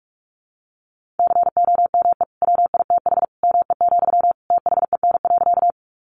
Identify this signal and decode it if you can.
Morse code, heard as a keyed tone, about 44 wpm, 700 Hz. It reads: COME WITH ME, THEN.
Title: Sentences from Top 200 Words 45wpm